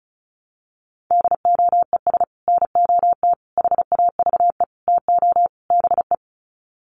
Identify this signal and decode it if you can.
Morse code, heard as a keyed tone, about 34 wpm, 700 Hz. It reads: DOES NOT HAVE TO BE
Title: Sets of 5 Words 35wpm